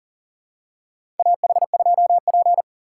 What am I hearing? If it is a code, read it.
Morse code, heard as a keyed tone, about 40 wpm, 700 Hz. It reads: AH2P
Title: USA Call Signs - 4 Characters - 4x Character Spacing - 40wpm